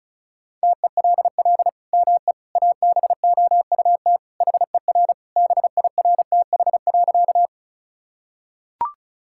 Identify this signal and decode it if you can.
Morse code, heard as a keyed tone, about 36 wpm, 700 Hz. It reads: TELL ME ABOUT HER BIRTH.
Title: Sentences from Top 1000 Words - Review Part 5 - 35wpm